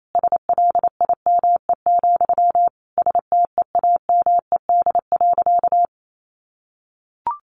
Morse code, heard as a keyed tone, about 28 wpm, 700 Hz. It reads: SLIME, STEAMED.
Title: A Sound Of Thunder 28wpm